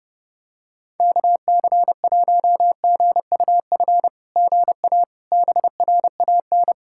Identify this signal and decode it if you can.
Morse code, heard as a keyed tone, about 31 wpm, 690 Hz. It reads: KC1GUF GA BRAN
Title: SOTA QSO Practice 30wpm